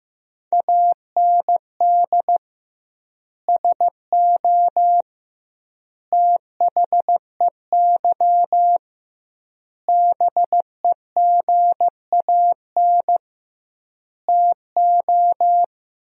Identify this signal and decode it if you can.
Morse code, heard as a keyed tone, about 15 wpm, 700 Hz. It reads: AND SO THEY BEGAN TO
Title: 2x Word-Spacing - Sentences from Top 400 Words - 15wpm